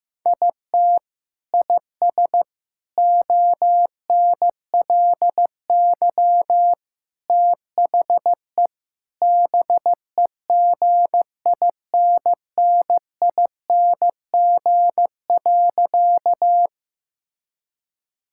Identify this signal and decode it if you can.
Morse code, heard as a keyed tone, about 15 wpm, 700 Hz. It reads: IT IS ONLY THE BEGINNING.
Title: Sentences from Top 1000 Words - Review Part 5 - 15wpm